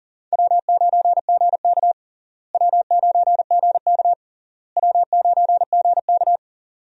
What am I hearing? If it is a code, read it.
Morse code, heard as a keyed tone, about 40 wpm, 700 Hz. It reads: W9GK W9GK W9GK
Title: USA Call Signs - 4 Characters - Repeated 3x - 40wpm